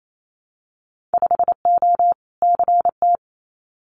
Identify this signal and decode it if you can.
Morse code, heard as a keyed tone, about 28 wpm, 700 Hz. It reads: 5O CT
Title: Winter Field Day Exchanges Encore 28wpm